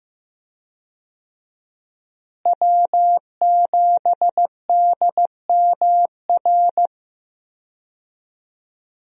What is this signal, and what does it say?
Morse code, heard as a keyed tone, about 15 wpm, 700 Hz. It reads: W7DMR
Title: USA Call Signs - 5 Characters - Repeated 3x - 15wpm